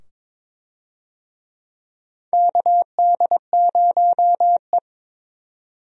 Morse code, heard as a keyed tone, about 22 wpm, 700 Hz. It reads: KD0E
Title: USA Call Signs - 4 Characters - 22wpm